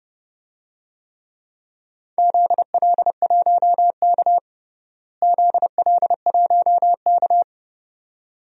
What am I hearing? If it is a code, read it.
Morse code, heard as a keyed tone, about 30 wpm, 700 Hz. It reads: ZL1K ZL1K
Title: DXCC-100 Call Signs - 4 Characters - Repeated 2x - 30wpm